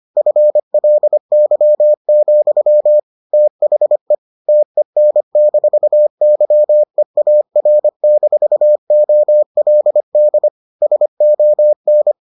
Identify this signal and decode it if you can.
Morse code, heard as a keyed tone, about 25 wpm, 600 Hz. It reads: FLY, THE TEN-YEAR-OLD SON